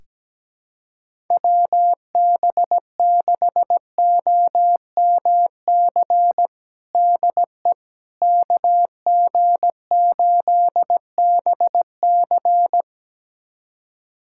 Morse code, heard as a keyed tone, about 17 wpm, 700 Hz. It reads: WB6OMC DE KG8BC
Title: USA Callsign DE Callsign 17wpm